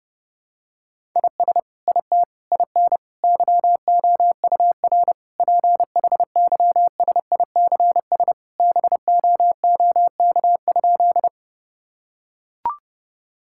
Morse code, heard as a keyed tone, about 32 wpm, 700 Hz. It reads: IS IT IN YOUR PHYSICS BOOK?
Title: Sentences from Top 1000 Words - Review Part 5 - 30wpm